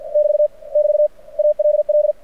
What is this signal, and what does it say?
Morse code, heard as a keyed tone, about 51 wpm, 600 Hz. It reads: <BT> <BT> ALL
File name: archived / CRSnet